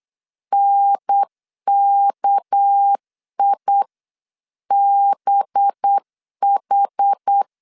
Morse code, heard as a keyed tone, about 8 wpm, 800 Hz. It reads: NKIBH
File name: 8wpm